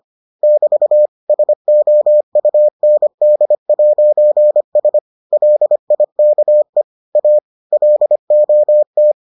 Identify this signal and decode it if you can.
Morse code, heard as a keyed tone, about 25 wpm, 600 Hz. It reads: <BT> SOUND'S LIKE A LOT